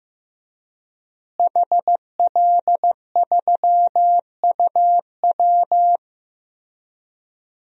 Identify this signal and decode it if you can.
Morse code, heard as a keyed tone, about 15 wpm, 700 Hz. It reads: HL3UW
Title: DXCC-100 Call Signs - 5 Characters - Repeated 2x - 15wpm